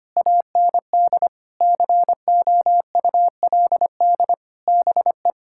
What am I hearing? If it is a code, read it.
Morse code, heard as a keyed tone, about 25 wpm, 700 Hz. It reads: AND COULD BE
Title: Sentences from Top 1000 Words - Review Part 6 - 25wpm